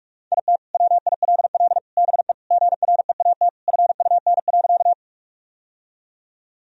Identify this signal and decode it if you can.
Morse code, heard as a keyed tone, about 42 wpm, 700 Hz. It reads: IT WILL BE GREAT FUN.
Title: Sentences from Top 1000 Words - Review Part 4 - 45wpm